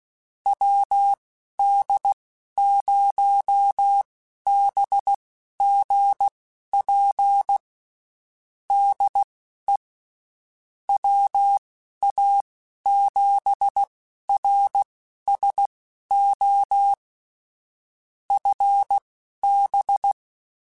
Morse code, heard as a keyed tone, about 16 wpm, spaced as for 11 wpm, 800 Hz. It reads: WD0BGP DE WA7RSO FB
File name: (spaced) / QSOs Spaced-mp3s / WD0BGP 1 @15 sp